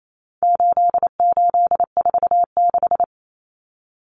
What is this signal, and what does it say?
Morse code, heard as a keyed tone, about 28 wpm, 700 Hz. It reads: 8846
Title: Numbers - 4-Digit 28wpm